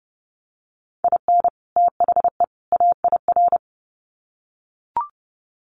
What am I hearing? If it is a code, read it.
Morse code, heard as a keyed tone, about 30 wpm, 700 Hz. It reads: IN THE AIR
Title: Sets of 3 Words 30wpm